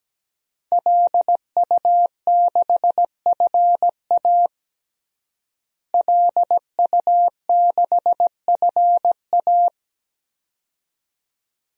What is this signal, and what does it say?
Morse code, heard as a keyed tone, about 17 wpm, 700 Hz. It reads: LU6FA LU6FA
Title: DXCC-100 Call Signs - 5 Characters - Repeated 2x - 17wpm